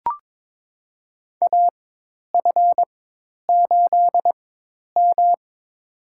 Morse code, heard as a keyed tone, about 22 wpm, spaced as for 10 wpm, 700 Hz. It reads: AF8M
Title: USA Call Signs - 4 Characters - 4x Character Spacing - 22wpm